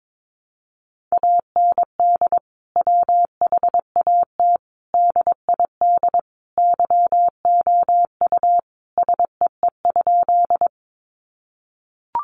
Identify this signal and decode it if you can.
Morse code, heard as a keyed tone, about 22 wpm, 700 Hz. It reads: AND WHAT DID YOU SEE?
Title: Sentences from Top 100 Words 22wpm